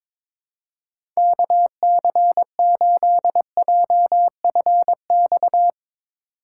Morse code, heard as a keyed tone, about 22 wpm, 700 Hz. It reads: KC8JFX
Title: USA Call Signs - 8x Character Spacing - 22wpm